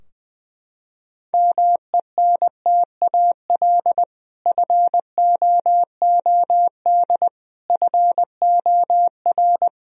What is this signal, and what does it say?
Morse code, heard as a keyed tone, about 20 wpm, 700 Hz. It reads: MENTAL FOOD FOR
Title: Sentences from Top 1000 Words - Review Part 4 - 20wpm